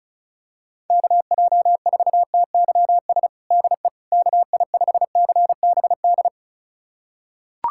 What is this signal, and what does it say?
Morse code, heard as a keyed tone, about 36 wpm, 700 Hz. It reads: KJ4TYS DE KI5CBD
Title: USA Callsign DE Callsign - 6 Characters - 35wpm